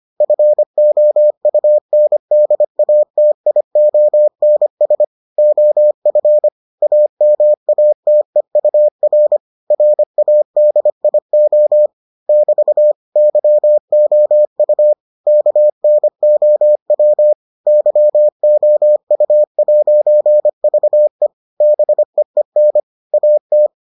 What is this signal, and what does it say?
Morse code, heard as a keyed tone, about 25 wpm, 600 Hz. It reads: FOUNDATIONS OF AMATEUR RADIO <BT> YOU KNOW YOU'VE BEEN AT